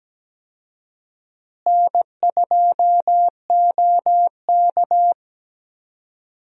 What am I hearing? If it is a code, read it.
Morse code, heard as a keyed tone, about 17 wpm, 700 Hz. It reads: N2OK
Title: USA Call Signs - 4 Characters - Repeated 3x - 17wpm